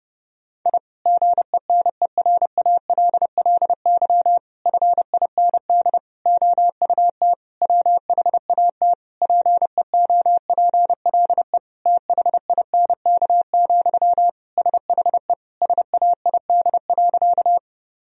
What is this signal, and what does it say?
Morse code, heard as a keyed tone, about 32 wpm, 700 Hz. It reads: I GENERALLY FIND OUT WHAT PEOPLE THINK, SHE SAID.
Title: Sentences from Top 1000 Words - Review Part 4 - 30wpm